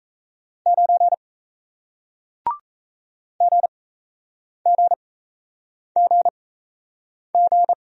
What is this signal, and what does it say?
Morse code, heard as a keyed tone, about 38 wpm, spaced as for 8 wpm, 700 Hz. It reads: 9 GGGG